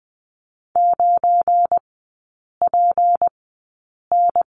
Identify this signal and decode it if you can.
Morse code, heard as a keyed tone, about 20 wpm, spaced as for 8 wpm, 700 Hz. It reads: 9PN
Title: 172 Lesson 28e - Number 9 - Three Characters 20wpm